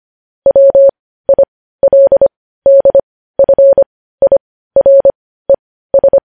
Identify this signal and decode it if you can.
Morse code, heard as a keyed tone, about 25 wpm, spaced as for 15 wpm, 550 Hz. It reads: WILDFIRES
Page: news headlines in morse code at 15 wpm